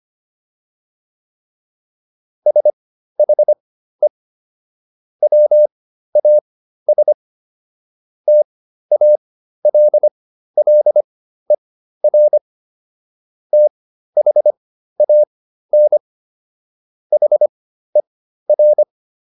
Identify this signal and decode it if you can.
Morse code, heard as a keyed tone, about 25 wpm, spaced as for 13 wpm, 600 Hz. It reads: SHE WAS TALLER THAN HER